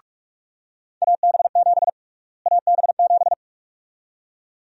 Morse code, heard as a keyed tone, about 46 wpm, 700 Hz. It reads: AB7 AB7
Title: USA Call Sign Prefixes - Repeated 2x - 45wpm